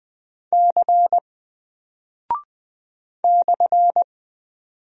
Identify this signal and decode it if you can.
Morse code, heard as a keyed tone, about 20 wpm, 700 Hz. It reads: C /
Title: Single Character (Letter:Number:Punctuation:Prosign) Alt 20wpm